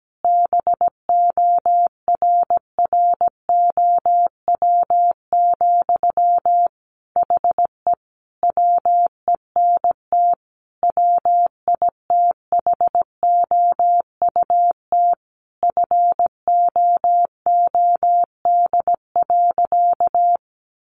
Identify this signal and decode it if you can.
Morse code, heard as a keyed tone, about 17 wpm, 700 Hz. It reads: BORROW, HE WENT WITHOUT FOOD.
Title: Sentences from Top 1500 Words 17wpm